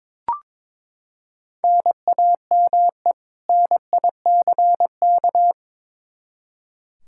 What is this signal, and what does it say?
Morse code, heard as a keyed tone, about 22 wpm, 700 Hz. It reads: NAME NICK
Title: Practice QSO - Exchange 1 - Beginning - 22wpm